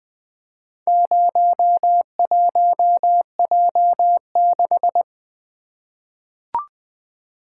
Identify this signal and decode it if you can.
Morse code, heard as a keyed tone, about 20 wpm, 700 Hz. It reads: 01J6
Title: Four Letter-Number Combinations - 20wpm